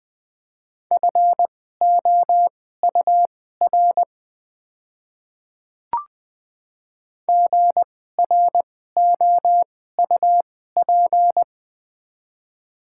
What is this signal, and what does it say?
Morse code, heard as a keyed tone, about 21 wpm, spaced as for 15 wpm, 700 Hz. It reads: FOUR GROUP
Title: 2x Character Spacing - Top 300 Words - 20wpm